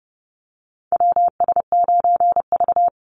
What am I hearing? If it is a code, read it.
Morse code, heard as a keyed tone, about 30 wpm, 700 Hz. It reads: WS9V
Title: Call Signs Repeated 3x - 30wpm